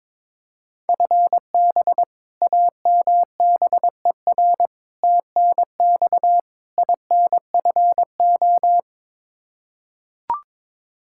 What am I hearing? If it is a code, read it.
Morse code, heard as a keyed tone, about 22 wpm, 700 Hz. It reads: FB AMBER TNX INFO
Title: Practice QSO - Exchange 3 - More Info - 22wpm